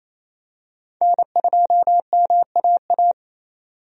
Morse code, heard as a keyed tone, about 28 wpm, 700 Hz. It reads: N2MAA
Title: USA Call Signs - 5 Characters - 28wpm